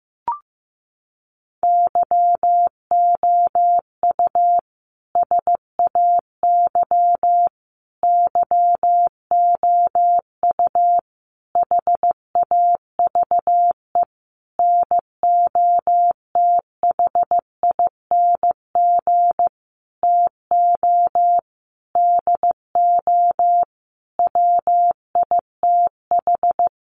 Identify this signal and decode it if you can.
Morse code, heard as a keyed tone, about 15 wpm, 700 Hz. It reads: YOU SAY YOU HAVE NOTHING TO DO WITH
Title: Sentences from Top 700 Words 15wpm